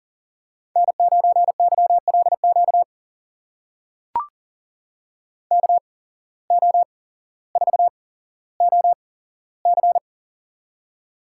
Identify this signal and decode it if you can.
Morse code, heard as a keyed tone, about 39 wpm, spaced as for 17 wpm, 700 Hz. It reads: N9YLQ KO4OC
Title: USA Call Signs - 5 Characters - 8x Character Spacing - 40wpm